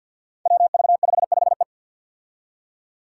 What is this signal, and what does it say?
Morse code, heard as a keyed tone, about 51 wpm, 700 Hz. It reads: WV55E
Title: Five Challenging Letter-Number Combinations 50wpm